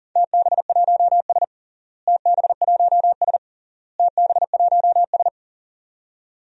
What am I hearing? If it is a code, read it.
Morse code, heard as a keyed tone, about 37 wpm, 700 Hz. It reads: TB1S TB1S TB1S
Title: DXCC-100 Call Signs - 4 Characters - Repeated 3x - 40wpm